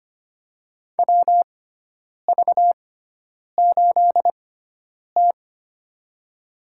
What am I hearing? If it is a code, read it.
Morse code, heard as a keyed tone, about 25 wpm, spaced as for 9 wpm, 700 Hz. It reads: WV8T
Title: USA Call Signs - 4 Characters - 6x Character Spacing - 25wpm